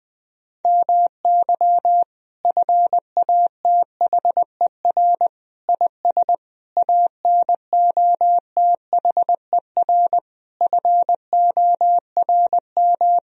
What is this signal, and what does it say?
Morse code, heard as a keyed tone, about 20 wpm, 700 Hz. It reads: MY FATHER IS ANOTHER FORM
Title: Sentences from Top 1000 Words - Review Part 1 - 20wpm